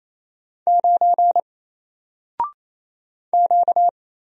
Morse code, heard as a keyed tone, about 27 wpm, 700 Hz. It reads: 9 Q
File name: Single-Letter-Number-28wpm